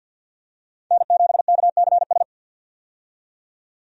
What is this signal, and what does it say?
Morse code, heard as a keyed tone, about 50 wpm, 700 Hz. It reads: N7KCS
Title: USA Call Signs - 5 Characters - 50wpm